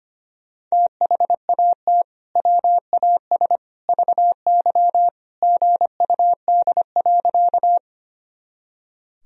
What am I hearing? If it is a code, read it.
Morse code, heard as a keyed tone, about 25 wpm, 700 Hz. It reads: THAT WAS VY GUD.
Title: Sentences from Top 200 Words and Abbreviations 25wpm